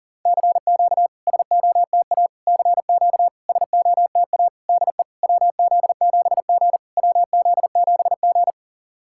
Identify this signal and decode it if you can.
Morse code, heard as a keyed tone, about 40 wpm, 700 Hz. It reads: CQ SOTA CQ SOTA DE WZ7G WZ7G
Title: SOTA QSO Practice 40wpm